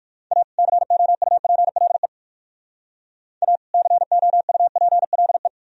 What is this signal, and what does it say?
Morse code, heard as a keyed tone, about 51 wpm, 700 Hz. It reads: A COUPLE A COUPLE
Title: Speed-Racing - Sets of 2 Words 45wpm